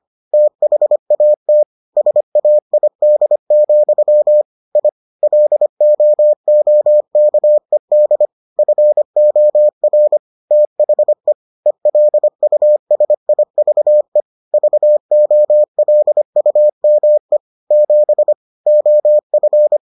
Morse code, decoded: THAT SAID, I LOOKED FOR THE ELUSIVE VOLUME 7 OF

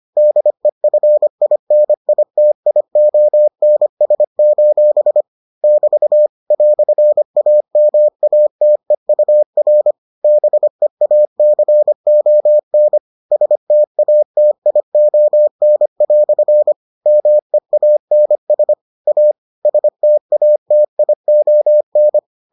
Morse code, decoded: DEFINITIONS: <BT> "AMATEUR BEACON STATION" MEANS A STATION

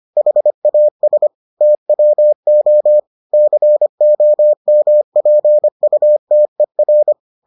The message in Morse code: HAS TWO COMPUTER

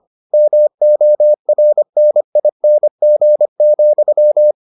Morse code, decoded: MORNING,

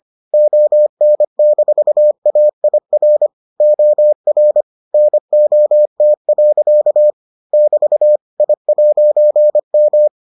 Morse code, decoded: ON-AIR OR NOT. <BT> I'M

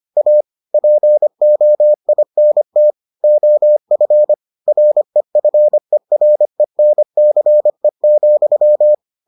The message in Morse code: A POINT OF REFERENCE,

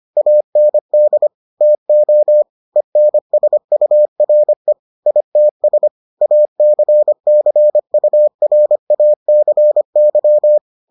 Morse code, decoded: AND TO ENSURE ITS ACCURACY

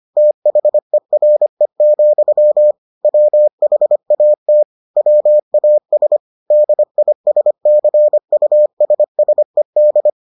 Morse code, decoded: THERE, WHAT WAS DISCUSSED

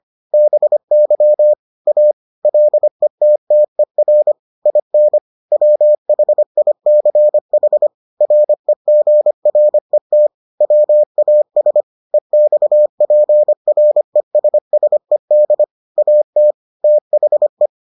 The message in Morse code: BY A LETTER IN WHICH REGRET WAS EXPRESSED AT THE

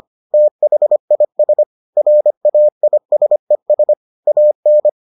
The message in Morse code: THIS RAISES AN

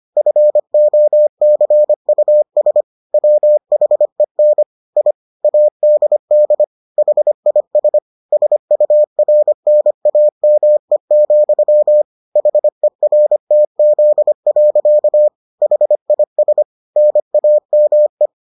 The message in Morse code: FOCUS WHEN I ADD HIS SURNAME, HERTZ. HIS NAME

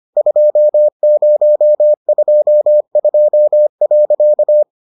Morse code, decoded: 2022.

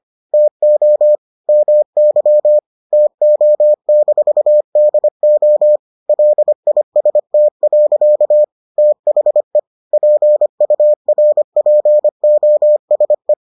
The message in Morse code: TO MY TO-DO LIST. THE PURPOSE

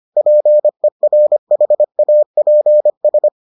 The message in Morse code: PERHAPS

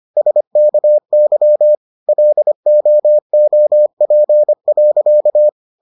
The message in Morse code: SKY LOOP.

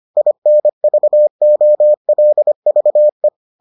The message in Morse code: INVOLVE

